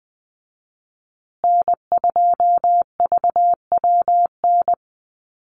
Morse code, heard as a keyed tone, about 20 wpm, 700 Hz. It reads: N2VWN
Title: USA Call Signs Repeated 3x 20wpm